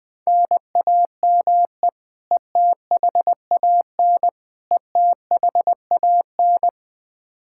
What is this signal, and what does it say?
Morse code, heard as a keyed tone, about 21 wpm, 700 Hz. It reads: NAME ETHAN ETHAN
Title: Practice QSO - Exchange 1 - Beginning - 20wpm